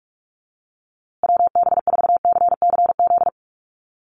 Morse code, heard as a keyed tone, about 44 wpm, 700 Hz. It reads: WB4CCZ
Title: USA Call Signs Repeated 2x 45wpm